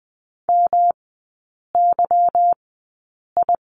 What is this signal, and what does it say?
Morse code, heard as a keyed tone, about 20 wpm, spaced as for 8 wpm, 700 Hz. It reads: MYI
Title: 166 Lesson 27e - Number 7 - Three Characters 20wpm